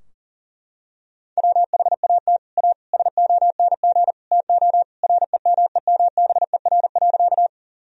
Morse code, heard as a keyed tone, about 40 wpm, 700 Hz. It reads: WHAT A SONG TO REMEMBER.
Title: Sentences from Top 1000 Words - Review Part 2 - 40wpm